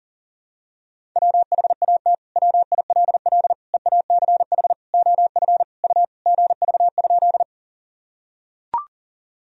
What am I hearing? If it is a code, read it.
Morse code, heard as a keyed tone, about 40 wpm, 700 Hz. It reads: WHAT WILL EACH OF U GV?
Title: Sentences from Top 200 Words and Abbreviations 40wpm